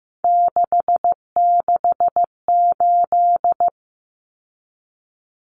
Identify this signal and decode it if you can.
Morse code, heard as a keyed tone, about 15 wpm, 700 Hz. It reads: 668
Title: Common Cut Numbers - 3-Digit 15wpm